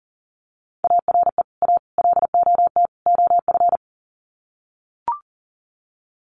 Morse code, heard as a keyed tone, about 40 wpm, 700 Hz. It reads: ARE A LOT OF
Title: Sets of 4 Words 40wpm